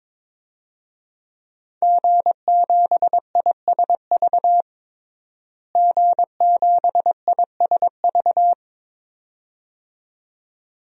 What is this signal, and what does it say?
Morse code, heard as a keyed tone, about 22 wpm, 700 Hz. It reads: G7ISV G7ISV
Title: DXCC-100 Call Signs - 5 Characters - Repeated 2x - 22wpm